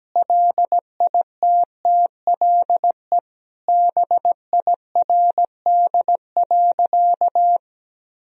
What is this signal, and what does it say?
Morse code, heard as a keyed tone, about 17 wpm, 700 Hz. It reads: LITTLE BIRD.
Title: Sentences from Top 1000 Words - Review Part 2 - 17wpm